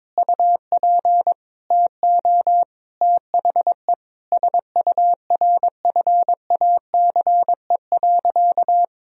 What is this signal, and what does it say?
Morse code, heard as a keyed tone, about 22 wpm, 700 Hz. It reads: UP TO THE SURFACE.
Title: Sentences from Top 1000 Words - Review Part 3 - 22wpm